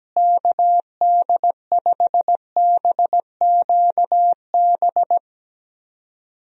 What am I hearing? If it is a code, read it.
Morse code, heard as a keyed tone, about 17 wpm, 700 Hz. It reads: KD5BQB
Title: USA Call Signs - 6 Characters - Repeated 2x - 17wpm